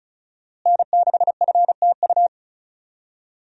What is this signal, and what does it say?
Morse code, heard as a keyed tone, about 36 wpm, 700 Hz. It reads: N6FTU
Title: USA Call Signs - 5 Characters - Repeated 3x - 35wpm